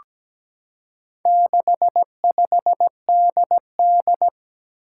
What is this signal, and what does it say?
Morse code, heard as a keyed tone, about 17 wpm, 700 Hz. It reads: 65DD
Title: Four Challenging Letter-Number Combinations 17wpm